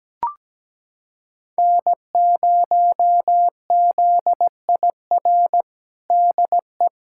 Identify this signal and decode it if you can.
Morse code, heard as a keyed tone, about 17 wpm, 700 Hz. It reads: N0ZIR DE
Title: USA Callsign DE Callsign - 5 Characters - 17wpm